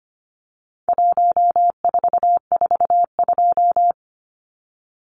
Morse code, heard as a keyed tone, about 25 wpm, 700 Hz. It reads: 1442
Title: Numbers - 4-Digit 25wpm